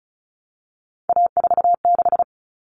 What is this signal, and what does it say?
Morse code, heard as a keyed tone, about 34 wpm, 700 Hz. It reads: A46
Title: Common Cut Numbers - 3-Digit 35wpm